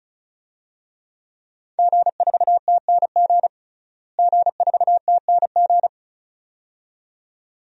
Morse code, heard as a keyed tone, about 36 wpm, 700 Hz. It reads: G4TNG G4TNG
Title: DXCC-100 Call Signs - 5 Characters - Repeated 2x - 35wpm